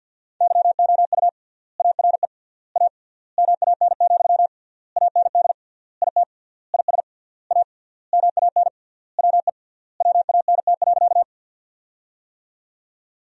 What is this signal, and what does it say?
Morse code, heard as a keyed tone, about 51 wpm, 700 Hz. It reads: YOU ARE A MAN, AND IT IS A MAN WE WANT.
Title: 3x Word-Spacing - Sentences from Top 200 Words - 50wpm